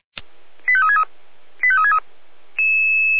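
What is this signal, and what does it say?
Five-tone selective call (ZVEI), heard as quick runs of five tones, tones of about 70 ms. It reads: 85363 85363
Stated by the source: ZVEI five-tone selective call received on an SDR.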